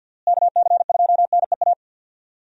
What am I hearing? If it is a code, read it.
Morse code, heard as a keyed tone, about 47 wpm, 700 Hz. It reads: KC2NEA
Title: USA Call Signs - 6 Characters - Repeated 2x - 50wpm